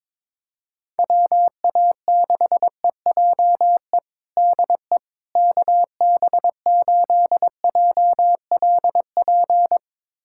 Morse code, heard as a keyed tone, about 22 wpm, 700 Hz. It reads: WA6EJE DE KB8JLP
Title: USA Callsign DE Callsign - 6 Characters - 22wpm